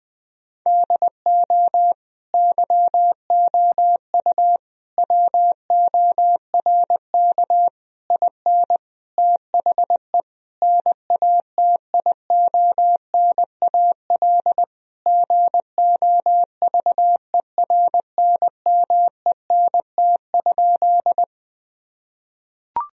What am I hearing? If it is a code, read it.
Morse code, heard as a keyed tone, about 21 wpm, 700 Hz. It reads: DO YOU WORK IN THE NATIONAL GOVERNMENT?
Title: Sentences from Top 1000 Words - Review Part 4 - 20wpm